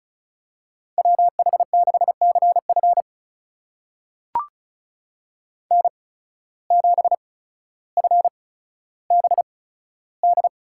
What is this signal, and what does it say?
Morse code, heard as a keyed tone, about 36 wpm, spaced as for 15 wpm, 700 Hz. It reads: WH6CF N7FBD
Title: USA Call Signs - 8x Character Spacing - 35wpm